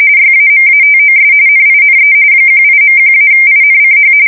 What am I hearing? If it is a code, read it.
RTTY, about 45.45 baud. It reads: YAMATOS NINE 18.1-INCH
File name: fsk170 / battleshipnj